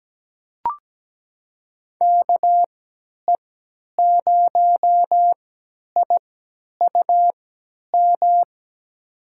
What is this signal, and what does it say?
Morse code, heard as a keyed tone, about 17 wpm, spaced as for 10 wpm, 700 Hz. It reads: KE0IUM
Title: USA Call Signs - 3x Character Spacing - 17wpm